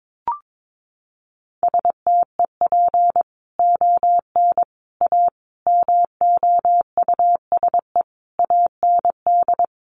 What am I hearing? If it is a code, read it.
Morse code, heard as a keyed tone, about 22 wpm, 700 Hz. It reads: STEP ON A MOUSE AND
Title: A Sound Of Thunder 22wpm